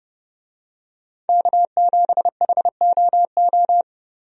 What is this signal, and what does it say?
Morse code, heard as a keyed tone, about 30 wpm, 700 Hz. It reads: K7HOO
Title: USA Call Signs - 5 Characters - Repeated 3x - 30wpm